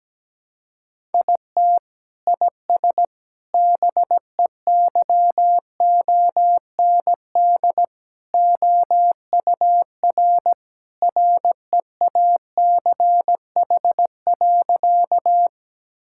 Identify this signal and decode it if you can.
Morse code, heard as a keyed tone, about 17 wpm, 700 Hz. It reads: IT IS BEYOND OUR REACH.
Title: Sentences from Top 1000 Words - Review Part 6 - 17wpm